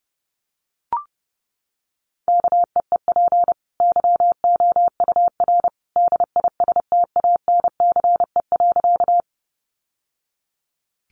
Morse code, heard as a keyed tone, about 30 wpm, 700 Hz. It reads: KEEP YOUR DISTANCE.
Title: Sentences from Top 1200 Words 30wpm